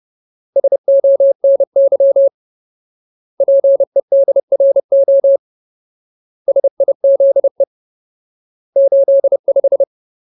Morse code, heard as a keyed tone, about 30 wpm, 550 Hz. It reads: SONY PEDRO SIZE 85
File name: Ultimate Word List / 30WPM Ultimate